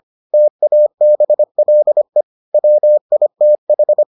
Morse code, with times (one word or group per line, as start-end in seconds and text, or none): 0.3-2.2 TABLE
2.5-4.0 WITH